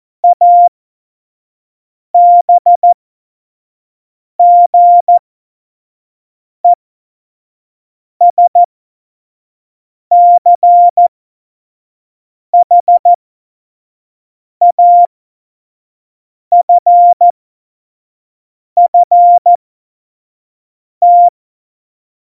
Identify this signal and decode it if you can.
Morse code, heard as a keyed tone, about 14 wpm, spaced as for 5 wpm, 700 Hz. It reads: ABGESCHAFFT
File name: Morsekurs / Witze / Witz 19. Januar 2021, 25 F70